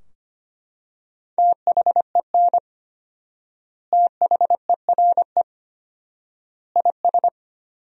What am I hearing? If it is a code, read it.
Morse code, heard as a keyed tone, about 25 wpm, 700 Hz. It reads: THEN THERE IS
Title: 4x Word-Spacing - Sentences from Top 100 Words - 25wpm